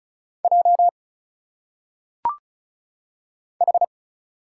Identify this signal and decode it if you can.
Morse code, heard as a keyed tone, about 34 wpm, 700 Hz. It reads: J H
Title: Single Letters - Phonetic Pronunciation 35wpm